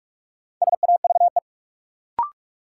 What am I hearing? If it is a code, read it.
Morse code, heard as a keyed tone, about 44 wpm, 700 Hz. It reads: SAVE
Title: Sentences from Top 1100 Words - New Word Drill Alt - 45wpm